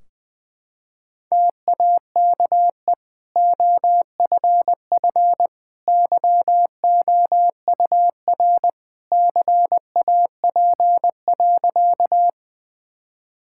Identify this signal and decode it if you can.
Morse code, heard as a keyed tone, about 20 wpm, 700 Hz. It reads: TAKE OFF YOUR CAP.
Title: Sentences from Top 1600 Words 20wpm